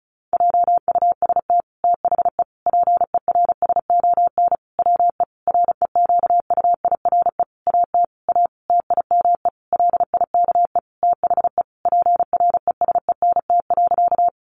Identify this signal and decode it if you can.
Morse code, heard as a keyed tone, about 34 wpm, 700 Hz. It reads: JUST THE PERSON WE REQUIRE AT A TIME LIKE THE PRESENT.
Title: Sentences from Top 900 Words 35wpm